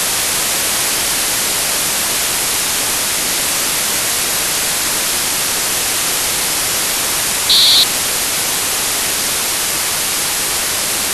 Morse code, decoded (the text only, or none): none